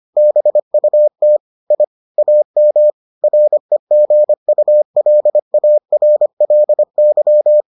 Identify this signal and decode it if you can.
Morse code, heard as a keyed tone, about 25 wpm, 600 Hz. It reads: BUT I AM REGULARLY